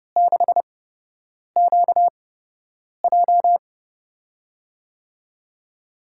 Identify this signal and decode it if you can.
Morse code, heard as a keyed tone, about 30 wpm, spaced as for 8 wpm, 700 Hz. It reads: 6QJ
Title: USA Call Signs - 4 Characters - 8x Character Spacing - 30wpm